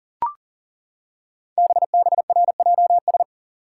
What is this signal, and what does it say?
Morse code, heard as a keyed tone, about 38 wpm, 690 Hz. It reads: BBRJS
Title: Code Groups - 5 Letters - 40wpm